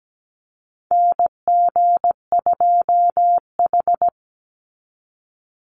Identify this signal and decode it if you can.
Morse code, heard as a keyed tone, about 17 wpm, 700 Hz. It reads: NG2H
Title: Call Signs - 17wpm